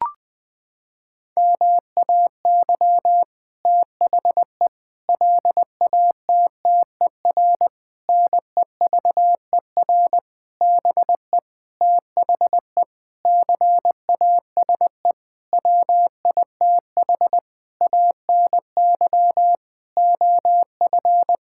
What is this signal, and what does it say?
Morse code, heard as a keyed tone, about 20 wpm, 700 Hz. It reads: MAY THE LATTER NEVER BE THE CASE WITH ANY OF
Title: Sentences from Top 1600 Words 20wpm